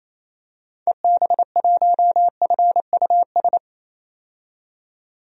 Morse code, heard as a keyed tone, about 27 wpm, 700 Hz. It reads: EB1FUS
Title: DXCC-100 Call Signs - 6 Characters - 28wpm